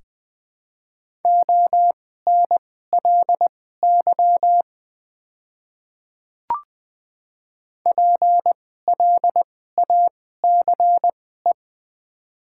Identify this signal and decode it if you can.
Morse code, heard as a keyed tone, about 20 wpm, spaced as for 14 wpm, 700 Hz. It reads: ONLY PLACE
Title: 2x Character Spacing - Top 200 Words - 20wpm